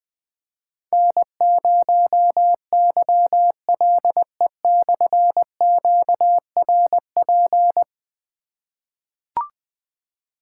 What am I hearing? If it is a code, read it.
Morse code, heard as a keyed tone, about 20 wpm, 700 Hz. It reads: N0YLE/QRP
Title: Difficult Call Signs 20wpm